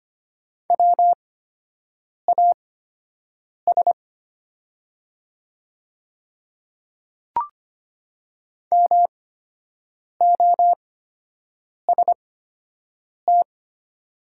Morse code, heard as a keyed tone, about 25 wpm, spaced as for 7 wpm, 700 Hz. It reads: WAS MOST
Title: 8x Character Spacing - Top 100 Words - 25wpm